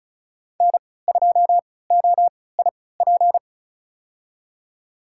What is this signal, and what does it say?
Morse code, heard as a keyed tone, about 36 wpm, spaced as for 20 wpm, 700 Hz. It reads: N2OIP
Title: USA Call Signs - 3x Character Spacing - 35wpm